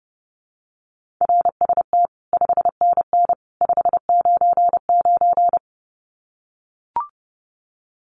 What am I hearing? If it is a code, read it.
Morse code, heard as a keyed tone, about 30 wpm, 700 Hz. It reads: RST 5NN 599
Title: Common QSO Phrases 30wpm